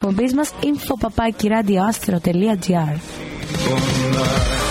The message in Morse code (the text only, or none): none